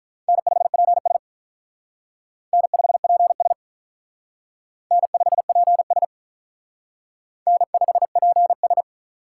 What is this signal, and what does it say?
Morse code, heard as a keyed tone, about 44 wpm, 700 Hz. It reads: N5PS N5PS N5PS N5PS